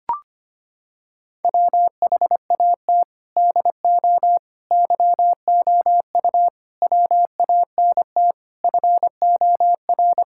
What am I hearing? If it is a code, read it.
Morse code, heard as a keyed tone, about 25 wpm, 700 Hz. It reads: WHAT DO YOU WANT FOR
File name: practice / 25wpm / Sentences with 4-Word Phrases - 25wpm